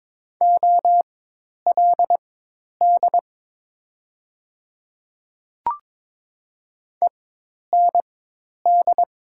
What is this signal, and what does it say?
Morse code, heard as a keyed tone, about 22 wpm, spaced as for 10 wpm, 700 Hz. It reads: OLD END
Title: 4x Character Spacing - Top 200 Words - 22wpm